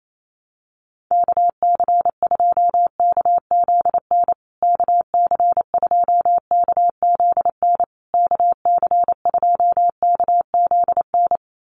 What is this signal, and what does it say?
Morse code, heard as a keyed tone, about 28 wpm, 690 Hz. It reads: KC2KZN KC2KZN KC2KZN
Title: USA Call Signs Repeated 3x 28wpm